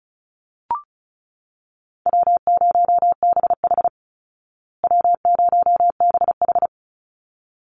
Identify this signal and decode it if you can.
Morse code, heard as a keyed tone, about 34 wpm, 700 Hz. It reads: W0BH W0BH
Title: Easy Call Signs Repeated 2x - 35wpm